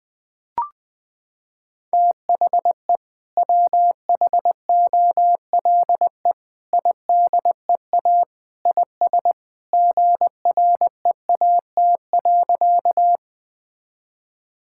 Morse code, heard as a keyed tone, about 21 wpm, 700 Hz. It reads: THE WHOLE IDEA IS GREAT.
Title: Sentences with 2-Word Phrases - 20wpm